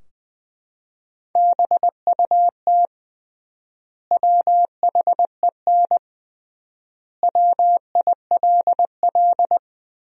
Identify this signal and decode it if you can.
Morse code, heard as a keyed tone, about 19 wpm, 700 Hz. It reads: BUT WHEN WILL
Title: 3x Word-Spacing - Sentences from Top 100 Words - 20wpm